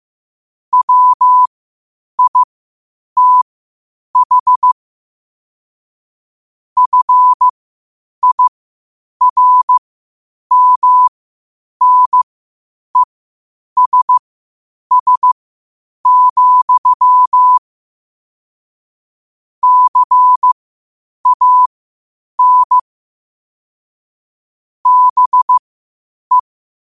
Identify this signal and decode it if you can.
Morse code, heard as a keyed tone, about 15 wpm, spaced as for 8 wpm, 1000 Hz. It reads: WITH FIRMNESS, CAN BE
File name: qotdfiles / 8wpm